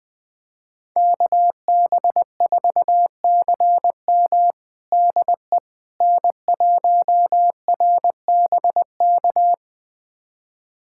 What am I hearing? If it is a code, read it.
Morse code, heard as a keyed tone, about 20 wpm, 700 Hz. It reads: KB4CM DE N1RBK
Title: USA Callsign DE Callsign - 5 Characters - 20wpm